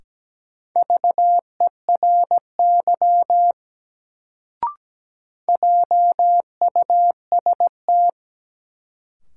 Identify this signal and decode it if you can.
Morse code, heard as a keyed tone, about 17 wpm, 700 Hz. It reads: VERY JUST
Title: Sentences from Top 200 Words - New Word Drill - 17wpm